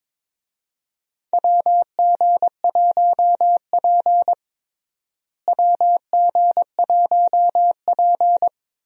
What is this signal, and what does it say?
Morse code, heard as a keyed tone, about 22 wpm, 700 Hz. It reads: WG1P WG1P
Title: USA Call Signs - 4 Characters - Repeated 2x - 22wpm